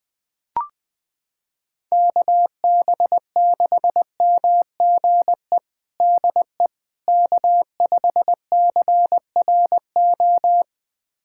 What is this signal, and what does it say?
Morse code, heard as a keyed tone, about 21 wpm, 700 Hz. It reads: KB6MGE DE K5CRO
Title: USA Callsign DE Callsign 20wpm